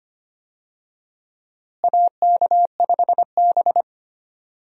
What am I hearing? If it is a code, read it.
Morse code, heard as a keyed tone, about 25 wpm, 700 Hz. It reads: AK5B
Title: USA Call Signs - 4 Characters - Repeated 2x - 25wpm